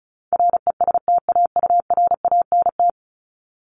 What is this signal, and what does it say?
Morse code, heard as a keyed tone, about 34 wpm, 700 Hz. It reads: RESTAURANT
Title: Top 1751-2000 Words 35wpm